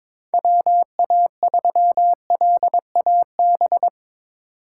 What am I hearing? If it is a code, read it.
Morse code, heard as a keyed tone, about 22 wpm, 700 Hz. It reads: WA3LAB
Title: K1USN SST QSO Practice - 22wpm